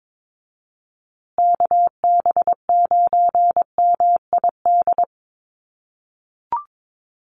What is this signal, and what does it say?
Morse code, heard as a keyed tone, about 22 wpm, 700 Hz. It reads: KB9MID
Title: USA Call Signs Repeated 2x 22wpm